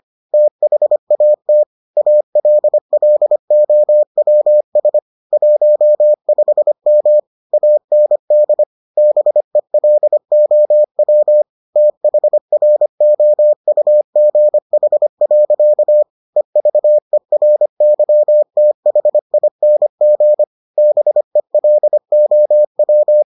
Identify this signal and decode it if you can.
Morse code, heard as a keyed tone, about 25 wpm, 600 Hz. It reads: THAT ALLOWS 15M AND BELOW THROUGH. EVERYTHING BELOW